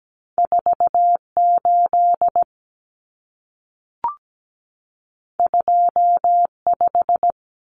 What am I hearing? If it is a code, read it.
Morse code, heard as a keyed tone, about 17 wpm, 700 Hz. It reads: 48 25
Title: Numbers - 2-Digit 17wpm